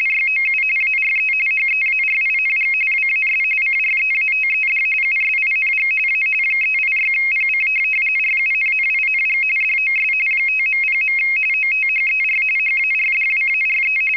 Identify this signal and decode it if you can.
RTTY, about 45.45 baud. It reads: YOU ARE HEARING WAS CREATED USING A 386PC RUNNING DOS 5.0. OTHER HARDWARE
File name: rtty425 / rtty45